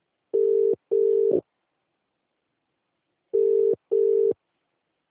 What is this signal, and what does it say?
Ringback tone (UK), 400 + 450 Hz.